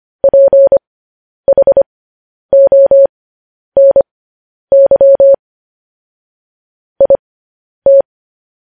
Morse code, heard as a keyed tone, about 25 wpm, spaced as for 10 wpm, 550 Hz. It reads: PHONY IT
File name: podcast / News headlines-1x-25x10